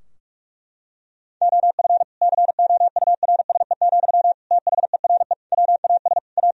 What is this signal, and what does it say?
Morse code, heard as a keyed tone, about 41 wpm, 700 Hz. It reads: OF COURSE, THERE WAS A
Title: Sentences from Top 1000 Words - Review Part 4 - 45wpm